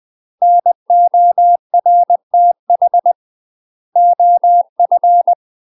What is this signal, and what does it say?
Morse code, heard as a keyed tone, about 20 wpm, 700 Hz. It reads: NORTH OF
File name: UK MASTER QSOs TRAINING/20WPM